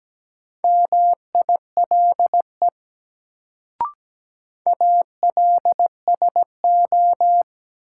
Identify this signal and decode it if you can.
Morse code, heard as a keyed tone, about 17 wpm, 700 Hz. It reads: MILE ALSO
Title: Sentences from Top 400 Words - New Word Drill - 17wpm